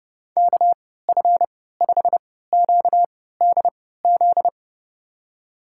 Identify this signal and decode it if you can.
Morse code, heard as a keyed tone, about 30 wpm, spaced as for 17 wpm, 700 Hz. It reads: KF5QDZ
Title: USA Call Signs - 6 Characters - 3x Character Spacing - 30wpm